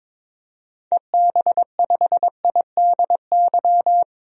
Morse code, heard as a keyed tone, about 22 wpm, 700 Hz. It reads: EB5IDY
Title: DXCC-100 Call Signs - 6 Characters - 22wpm